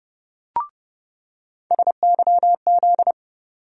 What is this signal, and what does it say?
Morse code, heard as a keyed tone, about 32 wpm, 700 Hz. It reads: SYZ
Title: USA Call Sign Postfixes - 3 Letters 30wpm